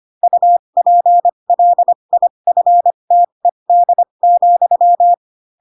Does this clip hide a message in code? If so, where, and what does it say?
Yes: Morse code from 0.2 to 5.1 s, heard as a keyed tone, about 25 wpm, spaced as for 22 wpm, 700 Hz. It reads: UPLIFTED,